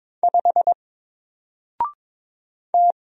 Morse code, heard as a keyed tone, about 22 wpm, 700 Hz. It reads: 5 T
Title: Single Character (Letter:Number:Punctuation:Prosign) 22wpm